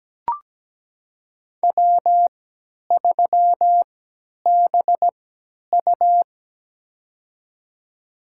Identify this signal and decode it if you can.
Morse code, heard as a keyed tone, about 17 wpm, spaced as for 10 wpm, 700 Hz. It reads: W3BU
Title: USA Call Signs - 4 Characters - 3x Character Spacing - 17wpm